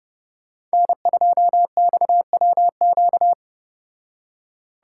Morse code, heard as a keyed tone, about 34 wpm, 700 Hz. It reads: N2XWQ